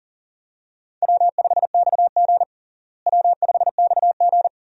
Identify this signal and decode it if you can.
Morse code, heard as a keyed tone, about 40 wpm, 700 Hz. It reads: W5XG W5XG
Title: USA Call Signs - 4 Characters - Repeated 2x - 40wpm